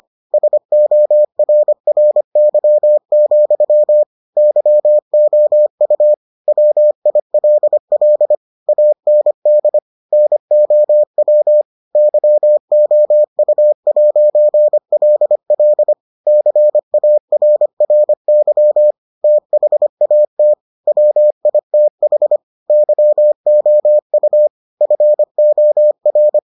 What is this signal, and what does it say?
Morse code, heard as a keyed tone, about 25 wpm, 600 Hz. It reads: SORRY, YOU WILL AND NOW YOU'LL CARRY THAT WITH YOU FOR